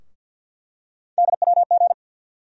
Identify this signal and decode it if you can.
Morse code, heard as a keyed tone, about 50 wpm, 690 Hz. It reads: DWG
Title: USA Call Sign Postfixes - 3 Letters 50wpm